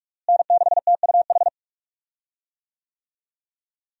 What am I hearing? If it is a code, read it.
Morse code, heard as a keyed tone, about 46 wpm, 700 Hz. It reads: N6TUH
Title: USA Call Signs - 5 Characters - Repeated 3x - 45wpm